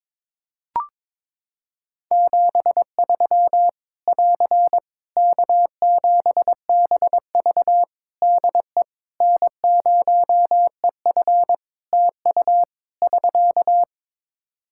Morse code, heard as a keyed tone, about 22 wpm, 700 Hz. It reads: 73 <AR> K7BV DE N0EF TU <SK>
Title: Practice QSO - Exchange 4 - Ending - 22wpm